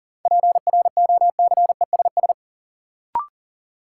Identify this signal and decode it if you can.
Morse code, heard as a keyed tone, about 40 wpm, 700 Hz. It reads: PROCESS
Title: Sentences from Top 1000 Words - Review Part 3 - New Word Drill - 40wpm